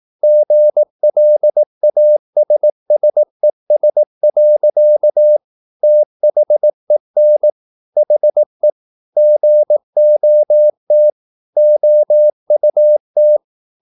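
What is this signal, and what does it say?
Morse code, heard as a keyed tone, about 18 wpm, 600 Hz. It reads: GLASSES. THEN HE GOT OUT
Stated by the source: Morse practice at 18 wpm, English text, from CW med Gnister.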